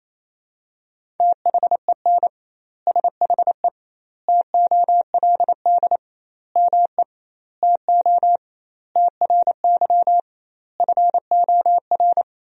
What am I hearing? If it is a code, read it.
Morse code, heard as a keyed tone, about 28 wpm, 700 Hz. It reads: THEN SHE TOLD ME TO TRY FOR
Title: 2x Word-Spacing - Sentences from Top 400 Words - 28wpm